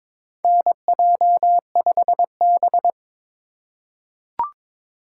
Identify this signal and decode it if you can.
Morse code, heard as a keyed tone, about 22 wpm, 700 Hz. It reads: NJ5B
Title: USA Call Signs - 4 Characters - 2x Character Spacing - 22wpm